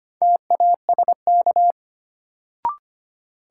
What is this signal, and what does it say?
Morse code, heard as a keyed tone, about 25 wpm, 700 Hz. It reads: TASK
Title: Sentences from Top 1000 Words - Review Part 7 - New Word Drill - 25wpm